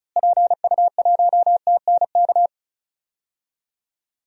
Morse code, heard as a keyed tone, about 36 wpm, 700 Hz. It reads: PU1TNK
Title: DXCC-100 Call Signs - 6 Characters - Repeated 3x - 35wpm